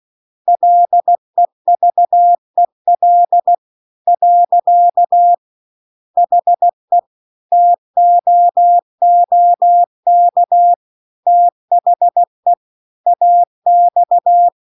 Morse code, heard as a keyed tone, about 16 wpm, 700 Hz. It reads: LEVEL . HE TOOK THE AX